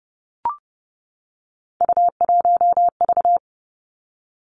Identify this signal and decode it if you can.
Morse code, heard as a keyed tone, about 31 wpm, 700 Hz. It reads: U1V
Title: Three Challenging Letter-Number Combinations 30wpm